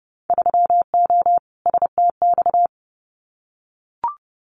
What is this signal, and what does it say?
Morse code, heard as a keyed tone, about 30 wpm, 700 Hz. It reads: 3O STX
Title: Winter Field Day Exchanges 30wpm